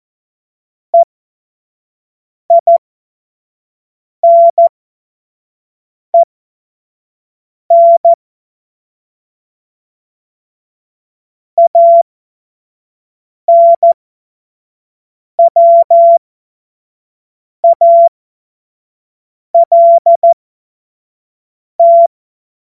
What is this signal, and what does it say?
Morse code, heard as a keyed tone, about 14 wpm, spaced as for 5 wpm, 680 Hz. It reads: EINEN ANWALT
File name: Morsekurs / Witze / Witz 03. Mai 2022 25 F70